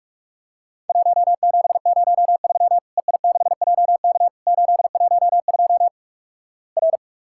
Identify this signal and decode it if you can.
Morse code, heard as a keyed tone, about 48 wpm, 700 Hz. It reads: 1703 EI6JK 812 R
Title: Worked All Europe DX Contest - QTC Exchange - 45wpm